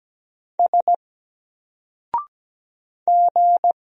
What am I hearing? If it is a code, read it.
Morse code, heard as a keyed tone, about 17 wpm, 700 Hz. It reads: S G
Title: Single Character (Letter:Number:Punctuation:Prosign) Alt 17wpm